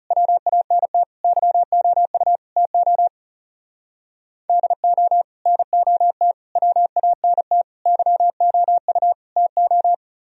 Morse code, heard as a keyed tone, about 37 wpm, 700 Hz. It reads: WANT YOU TO DO NOT WANT YOU TO
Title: Speed-Racing - Sets of 5 Words 30wpm